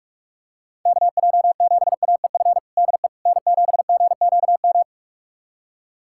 Morse code, heard as a keyed tone, about 46 wpm, 700 Hz. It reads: KJ7AEF DE N7GQM
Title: USA Callsign DE Callsign 45wpm